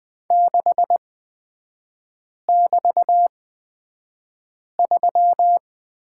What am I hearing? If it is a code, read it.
Morse code, heard as a keyed tone, about 20 wpm, spaced as for 5 wpm, 690 Hz. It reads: 6<BT>3
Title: 236 Lesson 38e - Letter q - Three Characters 20wpm